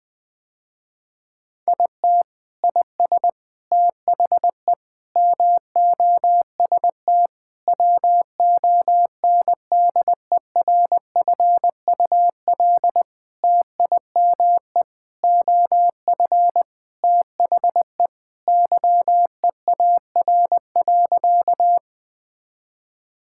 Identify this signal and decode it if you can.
Morse code, heard as a keyed tone, about 20 wpm, 700 Hz. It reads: IT IS THE MOST WONDERFUL TIME OF THE YEAR.
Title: Sentences with 8-Word Phrases - Spoken Beginning - 20wpm